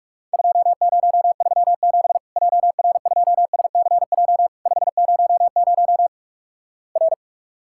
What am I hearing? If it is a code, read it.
Morse code, heard as a keyed tone, about 46 wpm, 700 Hz. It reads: 2037 JR2SCJ 500 R
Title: Worked All Europe DX Contest - QTC Exchange - 45wpm